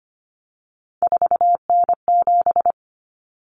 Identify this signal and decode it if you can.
Morse code, heard as a keyed tone, about 25 wpm, 700 Hz. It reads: 4N7
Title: Common Cut Numbers - 3-Digit 25wpm